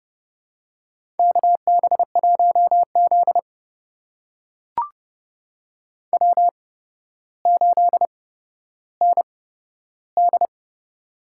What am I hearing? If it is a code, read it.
Morse code, heard as a keyed tone, about 30 wpm, spaced as for 13 wpm, 700 Hz. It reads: KB1Z W8ND
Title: USA Call Signs - 4 Characters - 8x Character Spacing - 30wpm